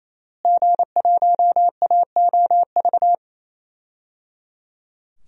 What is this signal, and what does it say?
Morse code, heard as a keyed tone, about 27 wpm, 700 Hz. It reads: G1AOV
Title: DXCC-100 Call Signs - 5 Characters - Repeated 3x - 28wpm